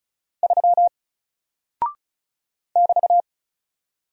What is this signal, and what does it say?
Morse code, heard as a keyed tone, about 36 wpm, 700 Hz. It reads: 3 <BT>
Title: Single Character (Letter:Number:Punctuation:Prosign) Alt 35wpm